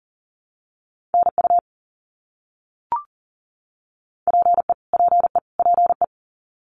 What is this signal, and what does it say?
Morse code, heard as a keyed tone, about 40 wpm, 700 Hz. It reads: NU PE PE PE
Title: Canadian Provinces and Territories - Repeated 3x - 40wpm